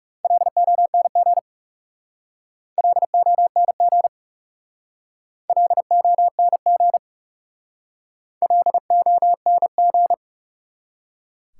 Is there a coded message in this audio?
Yes: Morse code from 0.2 to 10.1 s, heard as a keyed tone, about 40 wpm, 700 Hz. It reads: LONG LONG LONG LONG